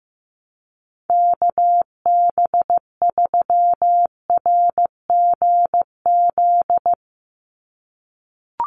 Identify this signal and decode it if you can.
Morse code, heard as a keyed tone, about 15 wpm, 700 Hz. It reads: KB3RGZ
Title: USA Call Signs 15wpm